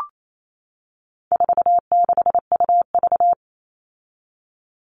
Morse code, heard as a keyed tone, about 28 wpm, 700 Hz. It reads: 46UV
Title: All Cut Numbers - 4-Digit 28wpm